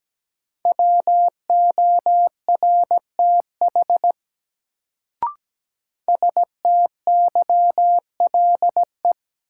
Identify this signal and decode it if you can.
Morse code, heard as a keyed tone, about 17 wpm, 700 Hz. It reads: WORTH STYLE
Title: Sentences from Top 800 Words - New Word Drill - 17wpm